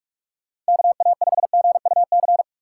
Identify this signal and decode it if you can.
Morse code, heard as a keyed tone, about 44 wpm, 700 Hz. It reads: KA5GUC